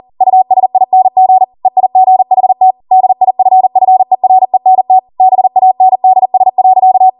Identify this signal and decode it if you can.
Morse code, heard as a keyed tone, about 40 wpm, 750 Hz. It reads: USING EIGHT DIFFERENT BANDS.